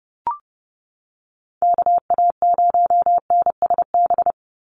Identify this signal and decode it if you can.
Morse code, heard as a keyed tone, about 30 wpm, 700 Hz. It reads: KA0NSB
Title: USA Call Signs 30wpm